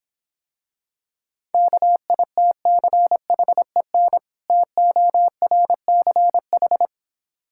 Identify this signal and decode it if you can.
Morse code, heard as a keyed tone, about 26 wpm, 700 Hz. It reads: KITCHEN TORCH